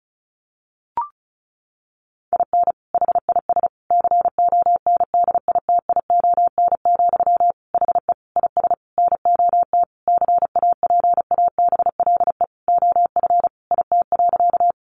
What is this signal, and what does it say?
Morse code, heard as a keyed tone, about 34 wpm, 700 Hz. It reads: IN HIS CONDITION, HE IS NOT CAPABLE OF IT.
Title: Sentences from Top 1200 Words 35wpm